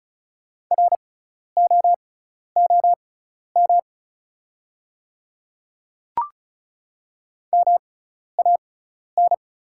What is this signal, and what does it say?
Morse code, heard as a keyed tone, about 36 wpm, spaced as for 12 wpm, 690 Hz. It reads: ROOM MAN
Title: 6x Character Spacing - Top 300 Words - 35wpm